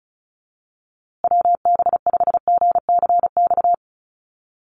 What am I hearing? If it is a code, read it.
Morse code, heard as a keyed tone, about 34 wpm, 700 Hz. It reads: WB5GCX
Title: USA Call Signs Repeated 2x 35wpm